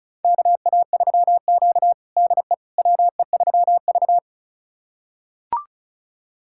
Morse code, heard as a keyed tone, about 36 wpm, 700 Hz. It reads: KA3Q DE WE3V
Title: USA Callsign DE Callsign - 4 Characters - 35wpm